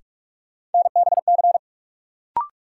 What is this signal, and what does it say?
Morse code, heard as a keyed tone, about 44 wpm, 700 Hz. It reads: NBC
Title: Sets of 5 Words Encore - New Word Drill - 45wpm